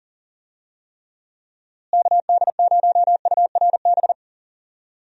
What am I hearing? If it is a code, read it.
Morse code, heard as a keyed tone, about 40 wpm, 700 Hz. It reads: KD0URB
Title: USA Call Signs - 6 Characters - Repeated 3x - 40wpm